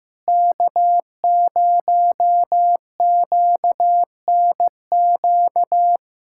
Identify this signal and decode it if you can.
Morse code, heard as a keyed tone, about 16 wpm, 700 Hz. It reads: K0QNQ
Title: USA Callsign DE Callsign - 5 Characters - 15wpm